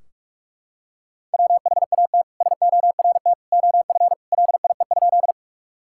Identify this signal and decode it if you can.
Morse code, heard as a keyed tone, about 42 wpm, 700 Hz. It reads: WHAT SORT OF LIE?
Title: Sentences from Top 1600 Words 45wpm